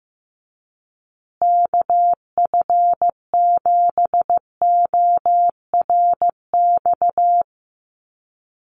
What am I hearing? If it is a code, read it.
Morse code, heard as a keyed tone, about 15 wpm, 700 Hz. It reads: KF7ORX
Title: USA Call Signs Repeated 3x 15wpm